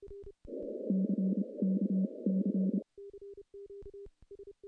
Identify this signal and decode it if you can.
Morse code, heard as a keyed tone, about 27 wpm, spaced as for 22 wpm, 190 Hz. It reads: CKC
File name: CW-SS-k5tr